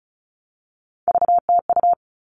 Morse code, heard as a keyed tone, about 34 wpm, 700 Hz. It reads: VTU
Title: All Cut Numbers - 3-Digit 35wpm